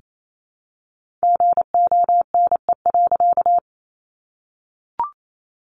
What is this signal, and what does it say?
Morse code, heard as a keyed tone, about 28 wpm, 700 Hz. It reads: GONE.
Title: A Sound Of Thunder 28wpm